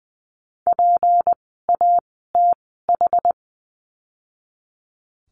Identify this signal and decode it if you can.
Morse code, heard as a keyed tone, about 19 wpm, spaced as for 14 wpm, 700 Hz. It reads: PATH
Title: 153 Lesson 25f - Letter p - Words 20wpm